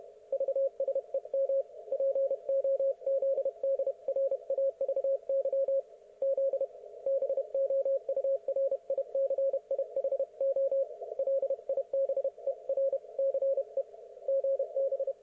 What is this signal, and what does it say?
Morse code, heard as a keyed tone, about 31 wpm, 530 Hz. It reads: VSEM POZDRAVY Z BOURICIHO LIBERCE GB
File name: Archiv / CUC107 / ok1fhi cul107